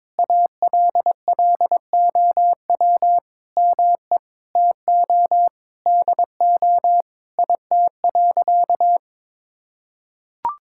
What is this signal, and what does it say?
Morse code, heard as a keyed tone, about 22 wpm, 700 Hz. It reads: ALLOW ME TO DO IT.
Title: Sentences from Top 1000 Words - Review Part 5 - 22wpm